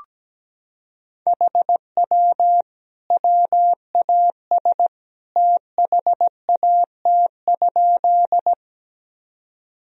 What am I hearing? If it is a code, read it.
Morse code, heard as a keyed tone, about 17 wpm, 700 Hz. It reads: HW WAS THAT?
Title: Sentences from Top 100 Words and Abbreviations 17wpm